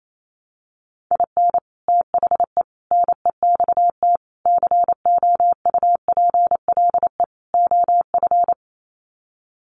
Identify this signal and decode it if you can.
Morse code, heard as a keyed tone, about 28 wpm, 700 Hz. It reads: IN THE NEXT COUPLE OF
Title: Sets of 5 Words Encore2 28wpm